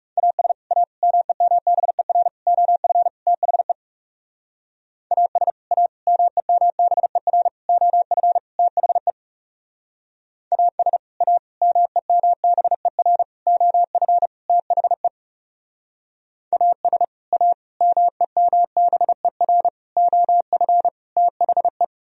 Morse code, heard as a keyed tone, about 37 wpm, 700 Hz. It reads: AS A MEMBER OF THE AS A MEMBER OF THE AS A MEMBER OF THE AS A MEMBER OF THE
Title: Speed-Racing - Sets of 5 Words 30wpm